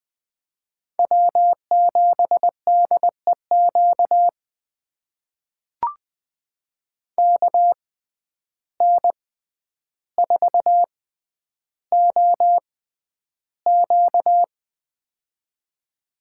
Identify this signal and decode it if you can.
Morse code, heard as a keyed tone, about 20 wpm, spaced as for 10 wpm, 700 Hz. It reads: W7DEQ KN4OQ
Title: USA Call Signs - 5 Characters - 6x Character Spacing - 20wpm